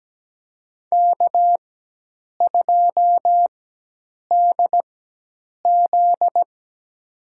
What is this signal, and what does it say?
Morse code, heard as a keyed tone, about 17 wpm, spaced as for 8 wpm, 700 Hz. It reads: K2DZ
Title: USA Call Signs - 4 Characters - 4x Character Spacing - 17wpm